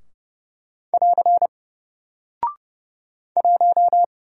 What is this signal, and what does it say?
Morse code, heard as a keyed tone, about 30 wpm, 700 Hz. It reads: <AR> 1
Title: Single Character (Letter:Number:Punctuation:Prosign) Alt 30wpm